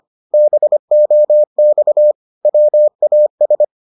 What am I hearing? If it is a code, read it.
Morse code, heard as a keyed tone, about 25 wpm, 600 Hz. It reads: BOX WAS